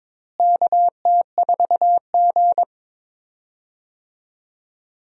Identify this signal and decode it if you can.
Morse code, heard as a keyed tone, about 22 wpm, 700 Hz. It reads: KT4G
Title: USA Call Signs - 4 Characters - Repeated 3x - 22wpm